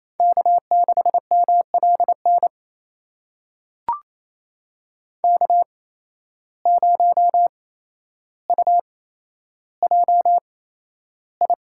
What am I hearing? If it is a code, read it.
Morse code, heard as a keyed tone, about 28 wpm, spaced as for 12 wpm, 700 Hz. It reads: K6MLN K0UJI
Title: USA Call Signs - 8x Character Spacing - 28wpm